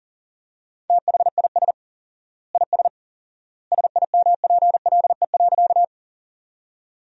Morse code, heard as a keyed tone, about 40 wpm, 700 Hz. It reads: THIS IS SIMPLE.
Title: 4x Word-Spacing - Sentences from Top 400 Words - 40wpm